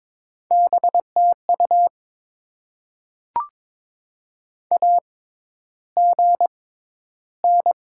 Morse code, heard as a keyed tone, about 22 wpm, spaced as for 11 wpm, 700 Hz. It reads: BTU AGN
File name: Top 20 QSO Elements - 6x Character Spacing - 22wpm